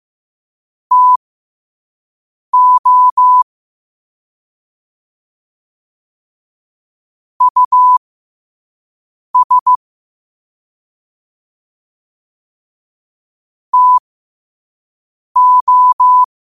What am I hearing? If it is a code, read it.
Morse code, heard as a keyed tone, about 15 wpm, spaced as for 5 wpm, 1000 Hz. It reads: TO US TO
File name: qotdfiles / 5wpm